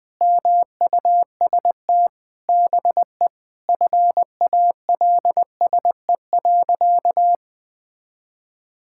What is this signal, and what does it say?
Morse code, heard as a keyed tone, about 21 wpm, 700 Hz. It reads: MUST BE FALSE.
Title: Sentences from Top 1600 Words 20wpm